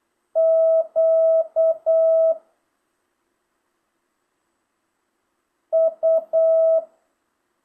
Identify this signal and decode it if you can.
Morse code, heard as a keyed tone, about 8 wpm, 650 Hz. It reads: Q U
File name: jh8cbh / 2025cw